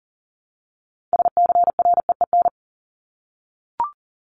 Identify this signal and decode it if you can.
Morse code, heard as a keyed tone, about 40 wpm, 700 Hz. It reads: SCREEN
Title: Top 751-1000 Words 40wpm